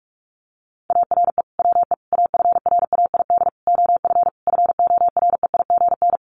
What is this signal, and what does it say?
Morse code, heard as a keyed tone, about 44 wpm, 700 Hz. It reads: ARE WE AFRAID OF FOREIGN
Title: Sentences from Top 1500 Words 45wpm